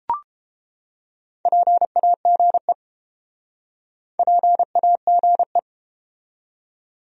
Morse code, heard as a keyed tone, about 32 wpm, 700 Hz. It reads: PAGE PAGE